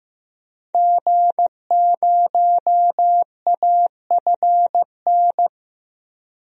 Morse code, heard as a keyed tone, about 15 wpm, 700 Hz. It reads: G0AFN
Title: DXCC-100 Call Signs - 5 Characters - 15wpm